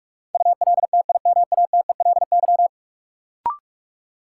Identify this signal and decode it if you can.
Morse code, heard as a keyed tone, about 42 wpm, 700 Hz. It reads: ULTIMATELY
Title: Sentences from Top 1200 Words - New Word Drill - 45wpm